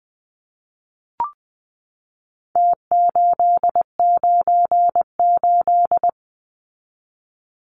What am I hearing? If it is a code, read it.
Morse code, heard as a keyed tone, about 20 wpm, 700 Hz. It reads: T898
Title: Numbers - Contest Style - 4-Digit 20wpm